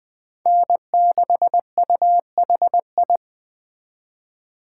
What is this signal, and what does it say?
Morse code, heard as a keyed tone, about 20 wpm, 700 Hz. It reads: N6UHI
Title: USA Call Signs - 5 Characters - 4x Character Spacing - 20wpm